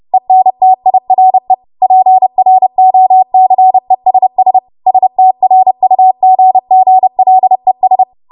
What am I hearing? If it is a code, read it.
Morse code, heard as a keyed tone, about 30 wpm, 750 Hz. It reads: ENTIRE PROCESS STRUGGLES
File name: Morse / 30WPM